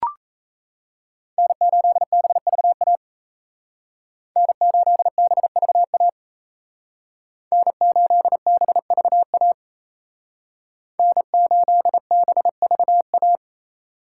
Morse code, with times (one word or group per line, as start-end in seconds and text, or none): 1.4-3.0 N8BVA
4.4-6.1 N8BVA
7.5-9.5 N8BVA
11.0-13.4 N8BVA